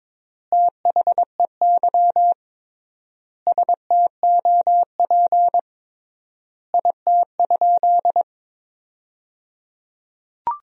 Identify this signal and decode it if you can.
Morse code, heard as a keyed tone, about 22 wpm, 700 Hz. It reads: THEY STOP IT?
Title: 3x Word-Spacing - Sentences from Top 300 Words - 22wpm